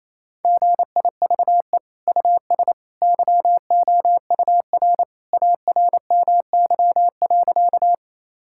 Morse code, heard as a keyed tone, about 28 wpm, 700 Hz. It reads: GIVE US YOUR ARMY.
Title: Sentences from Top 1000 Words - Review Part 5 - 28wpm